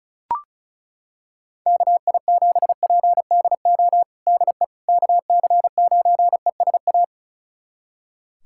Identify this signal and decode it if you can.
Morse code, heard as a keyed tone, about 34 wpm, 700 Hz. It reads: KI7PDO DE KC9ESA
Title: USA Callsign DE Callsign - 6 Characters - 35wpm